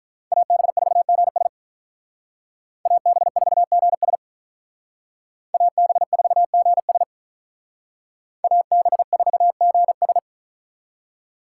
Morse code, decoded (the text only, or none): AB4GS AB4GS AB4GS AB4GS